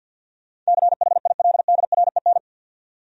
Morse code, heard as a keyed tone, about 60 wpm, 700 Hz. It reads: CHILDREN
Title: Speed-Racing Top 500 Words 50wpm